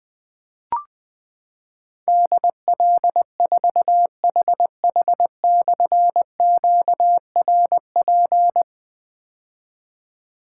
Morse code, heard as a keyed tone, about 20 wpm, 700 Hz. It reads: DL4HH/QRP
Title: Difficult Call Signs 20wpm